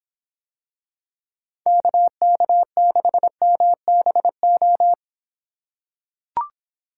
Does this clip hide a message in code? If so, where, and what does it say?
Yes: Morse code from 1.7 to 4.9 s, heard as a keyed tone, about 26 wpm, 700 Hz. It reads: KK6MBO